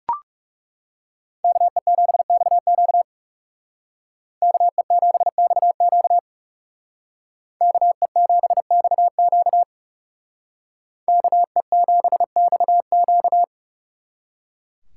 Morse code, heard as a keyed tone, about 37 wpm, 700 Hz. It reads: KE7XQ KE7XQ KE7XQ KE7XQ
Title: Speed Racing - USA 2x2 Call Signs 30wpm